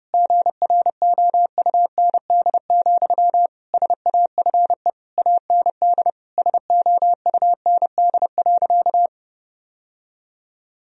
Morse code, heard as a keyed tone, about 30 wpm, 700 Hz. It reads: GROUND, SAFE AND SOUND.
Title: Sentences from Top 1000 Words - Review Part 2 - 30wpm